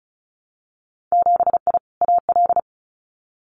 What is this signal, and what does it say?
Morse code, heard as a keyed tone, about 36 wpm, 700 Hz. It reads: 7I AL
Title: Winter Field Day Exchanges Encore 35wpm